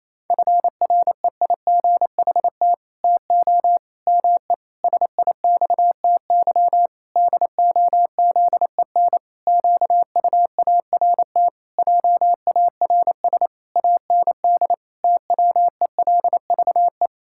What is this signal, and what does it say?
Morse code, heard as a keyed tone, about 27 wpm, 700 Hz. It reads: FREIGHT TO ME SIXTY DOZEN QUART JARS AND TWELVE
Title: Pangram Challenge v2 - 28wpm